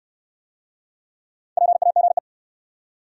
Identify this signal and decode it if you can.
Morse code, heard as a keyed tone, about 69 wpm, 700 Hz. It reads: PAGE